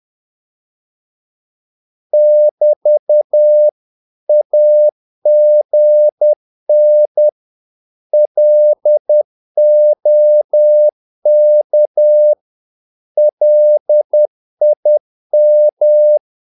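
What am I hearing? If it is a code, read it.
Morse code, heard as a keyed tone, about 10 wpm, 600 Hz. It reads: <BT> AGN LOK LIM
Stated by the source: Morse practice of short Danish words at 10 wpm, from CW med Gnister.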